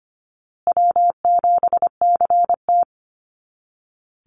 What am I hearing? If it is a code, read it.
Morse code, heard as a keyed tone, about 25 wpm, 700 Hz. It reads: W7CT
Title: Easy Call Signs Repeated 3x - 25wpm